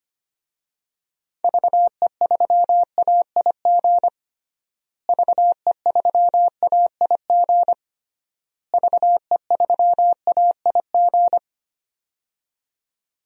Morse code, heard as a keyed tone, about 25 wpm, 700 Hz. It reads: VE3AIG VE3AIG VE3AIG
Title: DXCC-100 Call Signs - 6 Characters - Repeated 3x - 25wpm